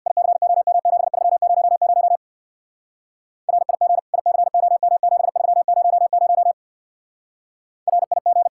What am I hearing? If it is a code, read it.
Morse code, heard as a keyed tone, about 63 wpm, 700 Hz. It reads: ICOM7300 RIG ICOM7300 RIG